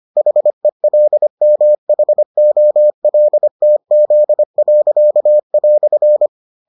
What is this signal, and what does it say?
Morse code, heard as a keyed tone, about 25 wpm, 600 Hz. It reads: HELMHOLTZ."